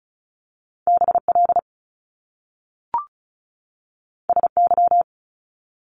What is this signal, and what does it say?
Morse code, heard as a keyed tone, about 36 wpm, 700 Hz. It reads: BL SY
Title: Top Two Letter Combinations in Words 35wpm